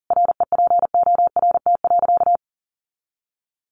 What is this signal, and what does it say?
Morse code, heard as a keyed tone, about 40 wpm, 700 Hz. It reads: REPORT.
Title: Sentences from Top 1300 Words 40wpm